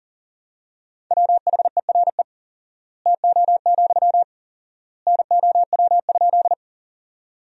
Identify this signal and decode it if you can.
Morse code, heard as a keyed tone, about 40 wpm, 700 Hz. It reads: WHERE TO, NOW?
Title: 4x Word-Spacing - Sentences from Top 200 Words - 40wpm